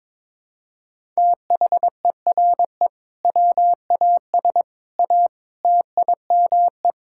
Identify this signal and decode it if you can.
Morse code, heard as a keyed tone, about 22 wpm, 700 Hz. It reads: THERE WAS A TIME
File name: practice / 22wpm / Build A Phrase - 5-Word Phrases - 22wpm